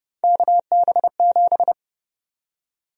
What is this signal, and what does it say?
Morse code, heard as a keyed tone, about 29 wpm, 700 Hz. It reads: KB7
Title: USA Call Sign Prefixes - 30wpm